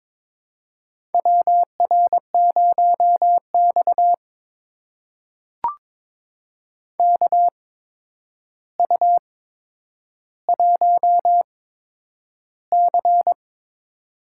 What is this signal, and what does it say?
Morse code, heard as a keyed tone, about 22 wpm, spaced as for 9 wpm, 700 Hz. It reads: WR0X KU1C
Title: USA Call Signs - 4 Characters - 8x Character Spacing - 22wpm